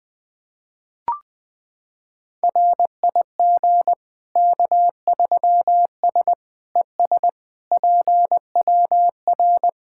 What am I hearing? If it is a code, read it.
Morse code, heard as a keyed tone, about 20 wpm, 700 Hz. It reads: RIG K3S ES PWR
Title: K1USN SST QSO Practice 20wpm